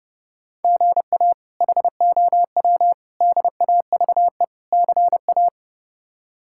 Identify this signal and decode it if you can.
Morse code, heard as a keyed tone, about 30 wpm, 700 Hz. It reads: GA HOW DAVE CA
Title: K1USN SST QSO Practice - 30wpm